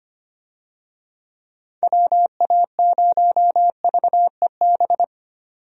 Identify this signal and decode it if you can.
Morse code, heard as a keyed tone, about 25 wpm, 700 Hz. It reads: WA0VEB
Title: USA Call Signs - 6 Characters - 25wpm